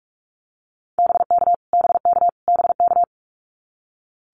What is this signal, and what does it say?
Morse code, heard as a keyed tone, about 44 wpm, 700 Hz. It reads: BK BK BK
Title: Abbreviations Repeated 3x - 45wpm